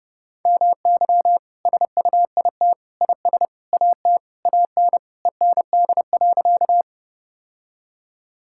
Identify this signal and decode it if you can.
Morse code, heard as a keyed tone, about 30 wpm, 700 Hz. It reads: MY SUIT IS AT AN END.
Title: Sentences from Top 1600 Words 30wpm